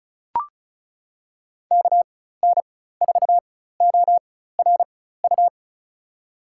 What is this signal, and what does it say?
Morse code, heard as a keyed tone, about 36 wpm, spaced as for 16 wpm, 700 Hz. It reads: KN4ORU
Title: USA Call Signs - 4x Character Spacing - 35wpm